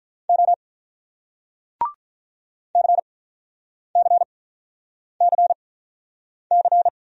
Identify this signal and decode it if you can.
Morse code, heard as a keyed tone, about 48 wpm, spaced as for 9 wpm, 700 Hz. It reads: O CCCC